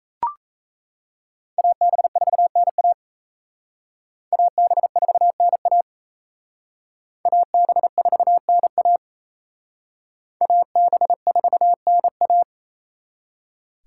Morse code, heard as a keyed tone, about 35 wpm, 700 Hz. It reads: AB4NA AB4NA AB4NA AB4NA
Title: Speed Racing - USA 2x2 Call Signs 28wpm